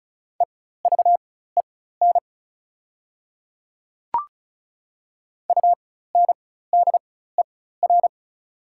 Morse code, heard as a keyed tone, about 33 wpm, spaced as for 16 wpm, 700 Hz. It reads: EVEN UNDER
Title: 4x Character Spacing - Top 200 Words - 35wpm